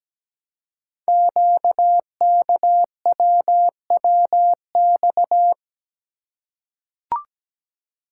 Morse code, heard as a keyed tone, about 17 wpm, 700 Hz. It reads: QKWWX
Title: Code Groups - 5 Letters - 17wpm